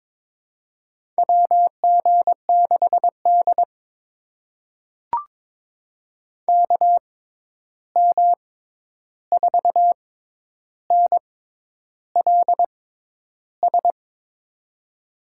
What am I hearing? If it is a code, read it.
Morse code, heard as a keyed tone, about 22 wpm, spaced as for 10 wpm, 700 Hz. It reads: WG6D KM4NLS
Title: USA Call Signs - 6x Character Spacing - 22wpm